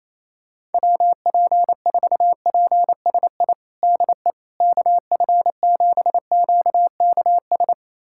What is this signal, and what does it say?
Morse code, heard as a keyed tone, about 28 wpm, 700 Hz. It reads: WP4PSI DE KF7QKS
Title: USA Callsign DE Callsign - 6 Characters - 28wpm